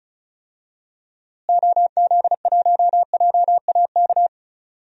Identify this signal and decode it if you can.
Morse code, heard as a keyed tone, about 36 wpm, 700 Hz. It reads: OZ1JAK
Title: DXCC-100 Call Signs - 6 Characters - Repeated 2x - 35wpm